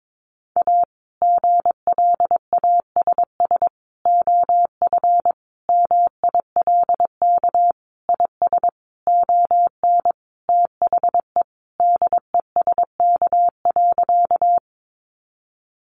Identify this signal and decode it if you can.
Morse code, heard as a keyed tone, about 22 wpm, 700 Hz. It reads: A GLASS OF MILK IS ON THE DESK.
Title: Additional Easy Phrases 22wpm